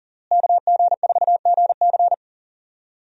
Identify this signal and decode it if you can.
Morse code, heard as a keyed tone, about 40 wpm, 700 Hz. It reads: KG4GC
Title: USA Call Signs - 5 Characters - Repeated 3x - 40wpm